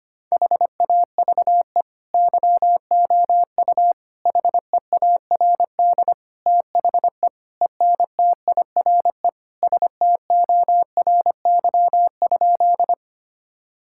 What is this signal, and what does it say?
Morse code, heard as a keyed tone, about 25 wpm, 700 Hz. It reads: HAVE YOU HEARD THE ENTIRE STORY?
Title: Sentences from Top 1000 Words - Review Part 5 - 25wpm